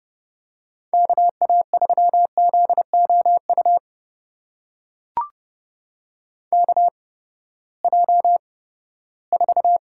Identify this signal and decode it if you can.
Morse code, heard as a keyed tone, about 30 wpm, spaced as for 17 wpm, 700 Hz. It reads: KA3ZOU KJ4
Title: USA Call Signs - 8x Character Spacing - 30wpm